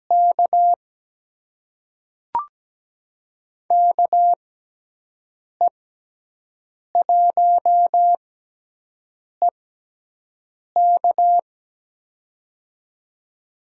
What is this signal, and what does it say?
Morse code, heard as a keyed tone, about 17 wpm, spaced as for 6 wpm, 700 Hz. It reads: K KE1EK
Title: USA Call Signs - 5 Characters - 6x Character Spacing - 17wpm